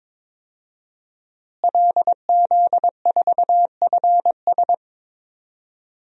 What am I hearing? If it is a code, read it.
Morse code, heard as a keyed tone, about 22 wpm, 700 Hz. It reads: LZ4FS
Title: DXCC-100 Call Signs - 5 Characters - Repeated 3x - 22wpm